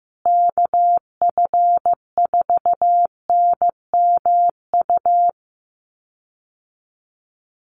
Morse code, heard as a keyed tone, about 15 wpm, 700 Hz. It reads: KF4NMU
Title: USA Call Signs Repeated 3x 15wpm